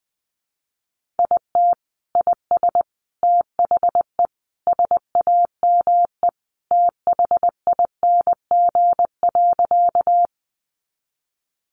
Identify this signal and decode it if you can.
Morse code, heard as a keyed tone, about 20 wpm, 700 Hz. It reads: IT IS THE SAME THING.
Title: Sentences from Top 200 Words 20wpm